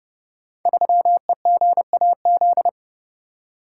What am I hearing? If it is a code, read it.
Morse code, heard as a keyed tone, about 31 wpm, 700 Hz. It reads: 3EGAZ
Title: Code Groups - 5 Letter:Numbers - 30wpm